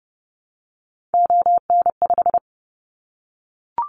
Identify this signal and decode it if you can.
Morse code, heard as a keyed tone, about 30 wpm, 700 Hz. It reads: ON5
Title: Callsign Prefixes for Top 50 DXCC - 30wpm